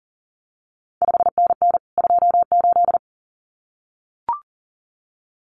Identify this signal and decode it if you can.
Morse code, heard as a keyed tone, about 41 wpm, 700 Hz. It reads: 5NN 28
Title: CQ WW DX Contest Exchange Encore - 40wpm